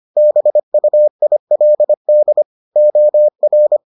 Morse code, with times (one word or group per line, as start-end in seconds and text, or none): 0.2-2.4 BUILD
2.8-3.8 OR